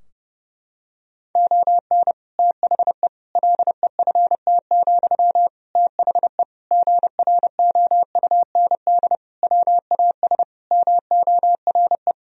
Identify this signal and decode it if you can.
Morse code, heard as a keyed tone, about 32 wpm, 700 Hz. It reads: ON THE LEFT, THE GROUND WAS MORE
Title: Sentences from Top 1000 Words - Review Part 4 - 30wpm